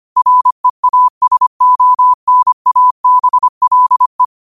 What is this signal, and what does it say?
Morse code, heard as a keyed tone, about 25 wpm, 1000 Hz. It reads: REASONABLE
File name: qotdfiles / 25wpm